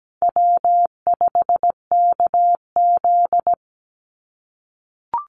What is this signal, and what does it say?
Morse code, heard as a keyed tone, about 17 wpm, 700 Hz. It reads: W5KZ
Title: USA Call Signs 17wpm